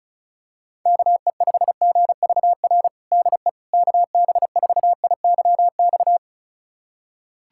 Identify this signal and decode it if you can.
Morse code, heard as a keyed tone, about 35 wpm, 700 Hz. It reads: KE5GVR DE KB4IYX
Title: USA Callsign DE Callsign 35wpm